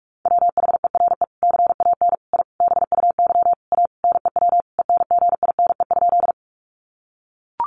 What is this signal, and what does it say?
Morse code, heard as a keyed tone, about 44 wpm, 700 Hz. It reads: WHERE CAN I BUY A NEW ENGINE?
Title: Sentences from Top 1500 Words 45wpm